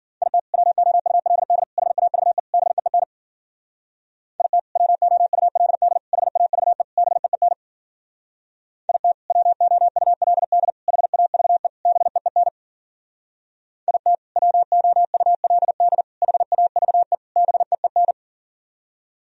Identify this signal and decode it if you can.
Morse code, heard as a keyed tone, about 56 wpm, 700 Hz. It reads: IT WOULD HAVE BEEN IT WOULD HAVE BEEN IT WOULD HAVE BEEN IT WOULD HAVE BEEN